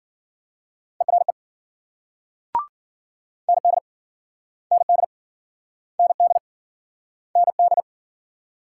Morse code, heard as a keyed tone, about 49 wpm, 700 Hz. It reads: ERE ND ND ND ND
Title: Speed-Racing Multi-letter Phonemes - 40wpm